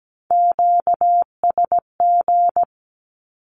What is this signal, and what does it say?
Morse code, heard as a keyed tone, about 17 wpm, 700 Hz. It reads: QSG
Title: Q-Codes with Spoken Meaning - 17wpm